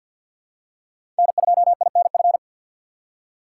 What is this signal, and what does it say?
Morse code, heard as a keyed tone, about 50 wpm, 700 Hz. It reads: N2INF
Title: USA Call Signs - 5 Characters - 3x Character Spacing - 50wpm